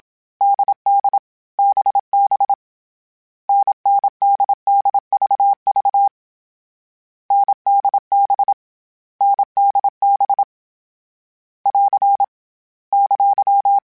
Morse code, heard as a keyed tone, about 27 wpm, 800 Hz. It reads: DD BB NNDDVV NDB NDB <AR> !
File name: Short Runs / Short Runs-mp3s / NDB NDB6 @25